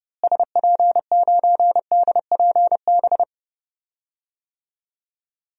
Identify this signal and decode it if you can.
Morse code, heard as a keyed tone, about 30 wpm, 700 Hz. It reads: SP9DPB
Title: DXCC-100 Call Signs - 6 Characters - Repeated 3x - 30wpm